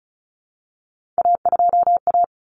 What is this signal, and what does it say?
Morse code, heard as a keyed tone, about 34 wpm, 700 Hz. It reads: A2A
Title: Common Cut Numbers - 3-Digit 35wpm